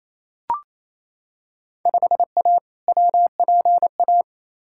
Morse code, heard as a keyed tone, about 28 wpm, 700 Hz. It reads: 5A WPA
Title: Field Day Exchanges Encore - 28wpm